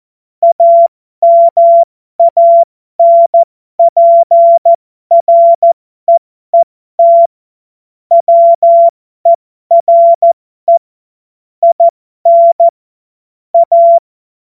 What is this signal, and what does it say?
Morse code, heard as a keyed tone, about 14 wpm, spaced as for 12 wpm, 680 Hz. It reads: AMANPREET WERE IN A
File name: Morsekurs / Witze